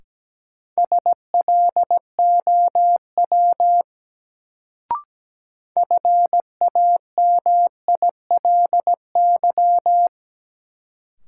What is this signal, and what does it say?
Morse code, heard as a keyed tone, about 17 wpm, 700 Hz. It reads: SLOW FAMILY
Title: Sentences from Top 400 Words - New Word Drill - 17wpm